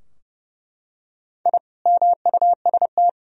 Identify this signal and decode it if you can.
Morse code, heard as a keyed tone, about 30 wpm, 700 Hz. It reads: I MUST
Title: Sentences from Top 1000 Words - Review Part 2 - 30wpm